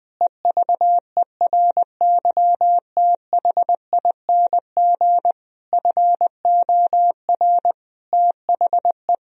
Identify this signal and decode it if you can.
Morse code, heard as a keyed tone, about 20 wpm, 700 Hz. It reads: EVERYTHING FOR THE
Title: Sentences from Top 1000 Words - Review Part 2 - 20wpm